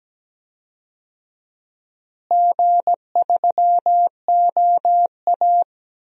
Morse code, heard as a keyed tone, about 17 wpm, 700 Hz. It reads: G3OA
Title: DXCC-100 Call Signs - 4 Characters - Repeated 3x - 17wpm